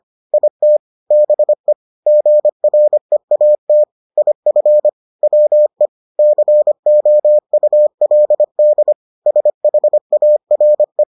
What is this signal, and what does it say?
Morse code, heard as a keyed tone, about 25 wpm, 600 Hz. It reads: IT BE GREAT IF WE COULD SHARE